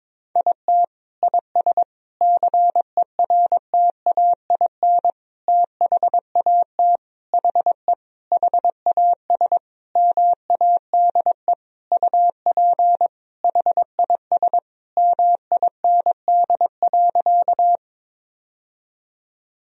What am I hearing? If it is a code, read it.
Morse code, heard as a keyed tone, about 22 wpm, 700 Hz. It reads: IT IS CERTAIN THAT HE HAS MADE UP HIS MIND.
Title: Sentences from Top 1000 Words - Review Part 2 - 22wpm